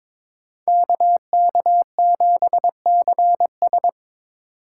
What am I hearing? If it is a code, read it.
Morse code, heard as a keyed tone, about 26 wpm, 700 Hz. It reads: KK7CS
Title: Speed Racing - USA Call Signs 22wpm